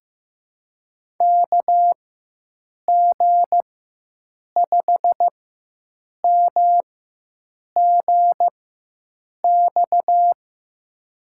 Morse code, heard as a keyed tone, about 15 wpm, spaced as for 7 wpm, 700 Hz. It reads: KG5MGX
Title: USA Call Signs - 6 Characters - 4x Character Spacing - 15wpm